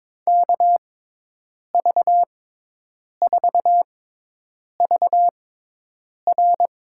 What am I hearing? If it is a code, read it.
Morse code, heard as a keyed tone, about 22 wpm, spaced as for 8 wpm, 700 Hz. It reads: KV4VR
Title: USA Call Signs - 5 Characters - 6x Character Spacing - 22wpm